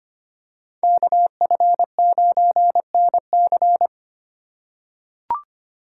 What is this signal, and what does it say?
Morse code, heard as a keyed tone, about 25 wpm, 700 Hz. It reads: KF9NC
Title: USA Call Signs - 5 Characters - 2x Character Spacing - 25wpm